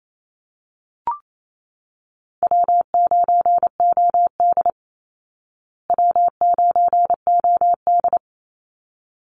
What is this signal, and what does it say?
Morse code, heard as a keyed tone, about 28 wpm, 690 Hz. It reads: W9OD W9OD
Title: Call Signs Repeated 3x - 28wpm